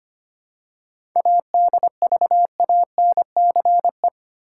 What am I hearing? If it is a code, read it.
Morse code, heard as a keyed tone, about 25 wpm, 700 Hz. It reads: ADVANCE
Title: Seven Letter Words - 25wpm